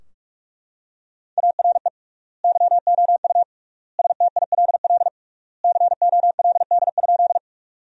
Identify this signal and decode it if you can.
Morse code, heard as a keyed tone, about 41 wpm, 700 Hz. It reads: ARE YOU STILL COLD?
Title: 3x Word-Spacing - Sentences from Top 400 Words - 45wpm